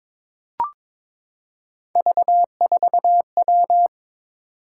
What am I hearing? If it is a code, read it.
Morse code, heard as a keyed tone, about 22 wpm, 700 Hz. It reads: V4W
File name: practice / 22wpm / Three Letter-Number Combinations - 22wpm